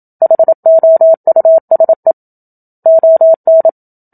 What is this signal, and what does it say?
Morse code, heard as a keyed tone, about 27 wpm, 650 Hz. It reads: HOUSE ON
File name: Book Two / 01-Chapter 1 - 27 WPM